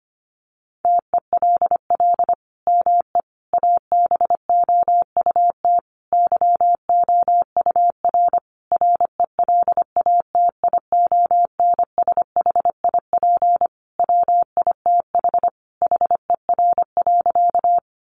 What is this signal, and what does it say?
Morse code, heard as a keyed tone, about 25 wpm, 700 Hz. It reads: TELL ME ABOUT YOUR RELATIONSHIP WITH HER.
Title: Sentences from Top 1100 Words 25wpm